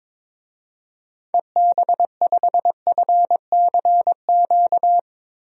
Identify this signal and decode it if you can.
Morse code, heard as a keyed tone, about 22 wpm, 700 Hz. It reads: EB5FCQ
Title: DXCC-100 Call Signs - 6 Characters - Repeated 2x - 22wpm